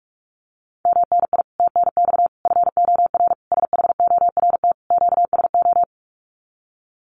Morse code, heard as a keyed tone, about 44 wpm, 700 Hz. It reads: MNI TNX FOR SHORT QSO
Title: Common QSO Phrases 45wpm